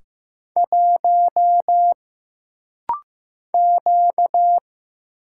Morse code, heard as a keyed tone, about 15 wpm, 700 Hz. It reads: 1 Q
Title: Single Letter-Number Alt - Rapid-Fire 15wpm